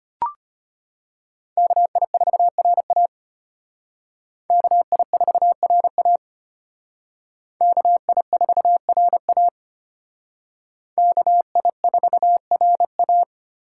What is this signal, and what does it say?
Morse code, heard as a keyed tone, about 32 wpm, 700 Hz. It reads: KI4RA KI4RA KI4RA KI4RA